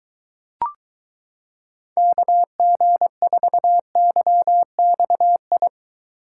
Morse code, heard as a keyed tone, about 22 wpm, 700 Hz. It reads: KG4YXI